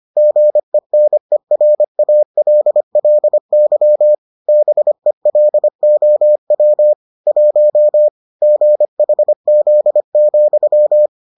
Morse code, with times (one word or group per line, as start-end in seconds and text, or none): 0.2-4.1 GENERALLY
4.5-6.9 BELOW
7.3-8.1 1
8.4-11.1 GHZ,